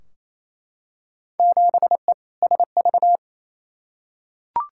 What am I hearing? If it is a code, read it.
Morse code, heard as a keyed tone, about 28 wpm, 700 Hz. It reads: 7E SV
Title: Field Day Exchanges Encore - 28wpm